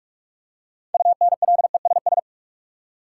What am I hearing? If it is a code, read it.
Morse code, heard as a keyed tone, about 42 wpm, 700 Hz. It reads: UNLESS
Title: Six Letter Words - 45wpm